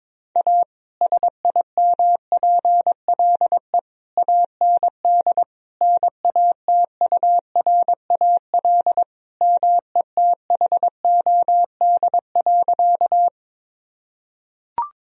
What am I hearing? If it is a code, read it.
Morse code, heard as a keyed tone, about 22 wpm, 700 Hz. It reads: A SIMPLE AND NATURAL METHOD.
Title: Sentences from Top 1000 Words - Review Part 4 - 22wpm